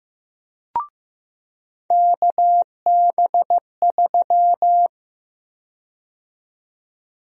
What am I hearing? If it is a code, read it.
Morse code, heard as a keyed tone, about 15 wpm, 700 Hz. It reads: KB3
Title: USA Call Sign Prefixes - 15wpm